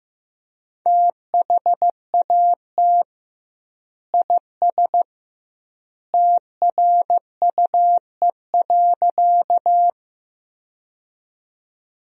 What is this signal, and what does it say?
Morse code, heard as a keyed tone, about 15 wpm, 700 Hz. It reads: THAT IS TRUE.
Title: 2x Word-Spacing - Sentences from Top 400 Words - 15wpm